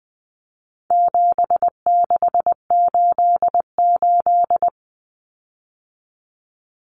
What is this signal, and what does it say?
Morse code, heard as a keyed tone, about 20 wpm, 700 Hz. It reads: 7688
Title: Common Cut Numbers - 4-Digit 20wpm